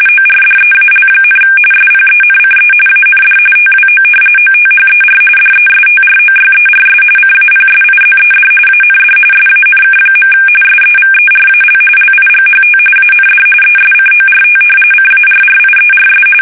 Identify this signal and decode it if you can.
RTTY, about 45.45 baud. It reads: ORDEAL, SPENCE AND TWO OTHER DESTROYERS CAPSIZED AND SANK. SEVERAL LIGHT AND ESCORT CARRIERS
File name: fsk850 / battleshipnj